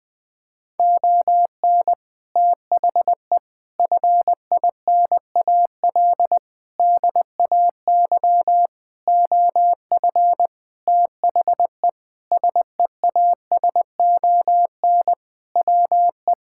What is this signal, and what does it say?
Morse code, heard as a keyed tone, about 21 wpm, 700 Hz. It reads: ON THE FINAL DAY OF THE SEASON WE
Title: Sentences with 7-Word Phrases - 20wpm